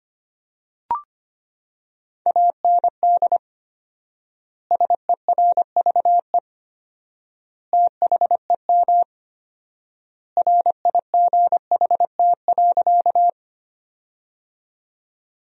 Morse code, heard as a keyed tone, about 25 wpm, 700 Hz. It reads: AND SERVE THEM RIGHT.
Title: 4x Word-Spacing - Sentences from Top 400 Words - 25wpm